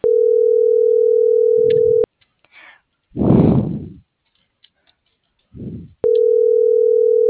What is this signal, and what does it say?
Ringback tone (North America), 440 + 480 Hz.